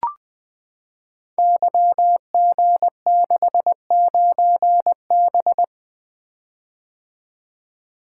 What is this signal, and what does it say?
Morse code, heard as a keyed tone, about 20 wpm, 700 Hz. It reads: YG69B
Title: Code Groups - 5 Letter:Numbers - 20wpm